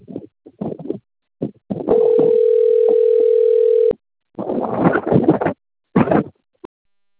Ringback tone (North America), 440 + 480 Hz.